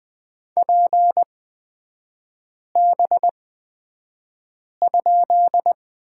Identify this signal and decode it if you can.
Morse code, heard as a keyed tone, about 20 wpm, spaced as for 5 wpm, 700 Hz. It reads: PB?
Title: 236 Lesson 38e - Letter q - Three Characters 20wpm